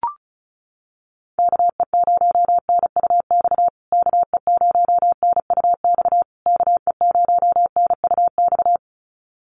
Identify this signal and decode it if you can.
Morse code, heard as a keyed tone, about 34 wpm, 700 Hz. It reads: KE0NUX KE0NUX KE0NUX
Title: USA Call Signs Repeated 3x 35wpm